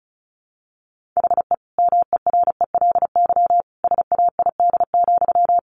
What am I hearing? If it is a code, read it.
Morse code, heard as a keyed tone, about 34 wpm, 700 Hz. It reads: HE MERELY SAID,
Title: Sentences from Top 1400 Words 35wpm